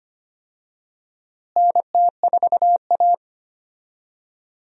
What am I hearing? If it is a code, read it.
Morse code, heard as a keyed tone, about 25 wpm, 700 Hz. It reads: NT4A
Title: USA Call Signs - 4 Characters - Repeated 2x - 25wpm